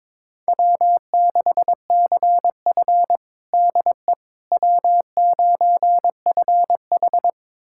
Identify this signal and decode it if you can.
Morse code, heard as a keyed tone, about 21 wpm, 700 Hz. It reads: W6CF DE W9FH
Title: USA Callsign DE Callsign - 4 Characters - 22wpm